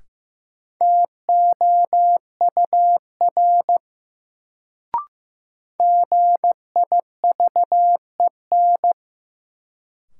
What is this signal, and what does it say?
Morse code, heard as a keyed tone, about 16 wpm, 700 Hz. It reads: TOUR GIVEN
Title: Sentences from Top 1400 Words - New Word Drill - 15wpm